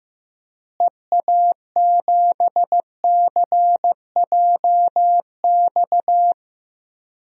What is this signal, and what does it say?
Morse code, heard as a keyed tone, about 15 wpm, 700 Hz. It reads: EA7CJX
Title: DXCC-100 Call Signs - 6 Characters - Repeated 3x - 15wpm